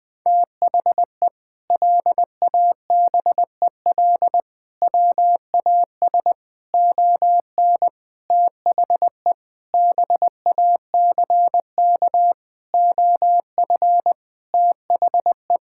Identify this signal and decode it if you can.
Morse code, heard as a keyed tone, about 20 wpm, 700 Hz. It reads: THE LABEL WAS ON THE BACK OF THE
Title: Sentences with 4-Word Phrases - 20wpm